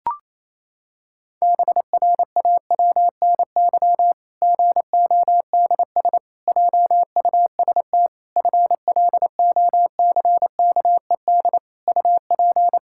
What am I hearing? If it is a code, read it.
Morse code, heard as a keyed tone, about 28 wpm, 700 Hz. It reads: BRAWNY GODS JUST FLOCKED UP
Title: Pangram Challenge v2 - 28wpm